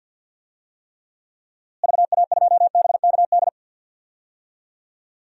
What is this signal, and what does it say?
Morse code, heard as a keyed tone, about 46 wpm, 700 Hz. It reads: VA2BKD
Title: DXCC-100 Call Signs - 6 Characters - Repeated 2x - 50wpm